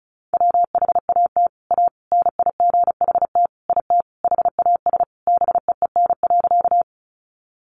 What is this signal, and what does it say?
Morse code, heard as a keyed tone, about 34 wpm, 700 Hz. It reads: WHAT A NIGHT IT HAS BEEN.
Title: Sentences from Top 300 Words 35wpm